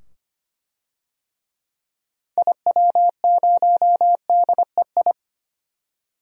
Morse code, heard as a keyed tone, about 25 wpm, 700 Hz. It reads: IW0DEI
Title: DXCC-100 Call Signs - 6 Characters - 25wpm